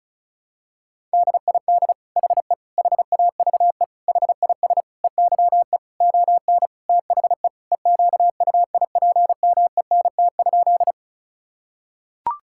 Morse code, heard as a keyed tone, about 36 wpm, 700 Hz. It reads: DID HE HAVE HIS EYE ON THE EQUIPMENT?
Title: Sentences from Top 1000 Words - Review Part 5 - 35wpm